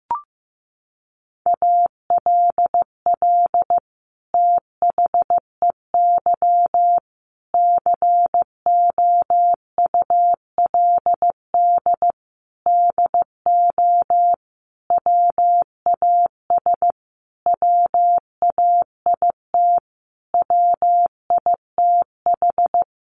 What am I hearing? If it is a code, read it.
Morse code, heard as a keyed tone, about 15 wpm, 700 Hz. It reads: ALL THEY COULD DO WAS WAIT WITH
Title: Sentences from Top 1300 Words 15wpm